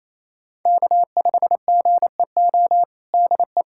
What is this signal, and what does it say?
Morse code, heard as a keyed tone, about 28 wpm, 700 Hz. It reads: K5GEO DE
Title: USA Callsign DE Callsign - 5 Characters - 28wpm